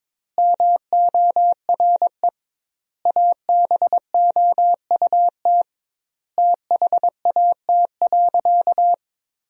Morse code, heard as a keyed tone, about 22 wpm, 700 Hz. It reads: MORE ABOUT THAT.
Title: 2x Word-Spacing - Sentences from Top 200 Words - 22wpm